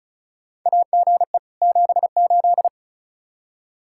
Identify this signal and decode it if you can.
Morse code, heard as a keyed tone, about 34 wpm, 700 Hz. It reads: AGE 78
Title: K1USN SST QSO Practice 35wpm